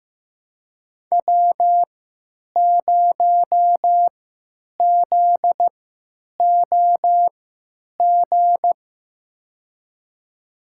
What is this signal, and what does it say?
Morse code, heard as a keyed tone, about 15 wpm, spaced as for 8 wpm, 700 Hz. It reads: W0ZOG
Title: USA Call Signs - 5 Characters - 3x Character Spacing - 15wpm